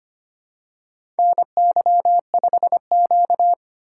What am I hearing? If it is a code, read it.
Morse code, heard as a keyed tone, about 25 wpm, 700 Hz. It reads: NY5Q
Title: USA Call Signs - 4 Characters - 3x Character Spacing - 25wpm